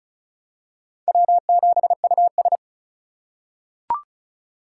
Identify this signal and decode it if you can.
Morse code, heard as a keyed tone, about 34 wpm, 700 Hz. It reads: W7US
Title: USA Call Signs - 3x Character Spacing - 35wpm